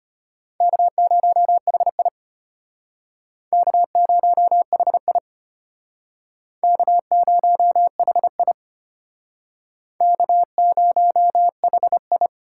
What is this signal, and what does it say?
Morse code, heard as a keyed tone, about 32 wpm, 700 Hz. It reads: K0HI K0HI K0HI K0HI